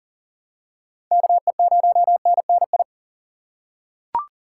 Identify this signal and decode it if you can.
Morse code, heard as a keyed tone, about 40 wpm, 700 Hz. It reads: KE0NNI
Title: USA Call Signs - 6 Characters - 4x Character Spacing - 40wpm